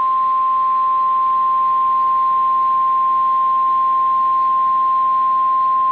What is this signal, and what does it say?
Emergency Alert System (SAME) tones (the 1050 Hz attention tone). It is heard from 1.1 to 5.3 s.